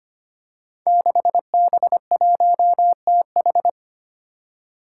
Morse code, heard as a keyed tone, about 25 wpm, 700 Hz. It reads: 6B1TH
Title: Five Challenging Letter-Number Combinations 25wpm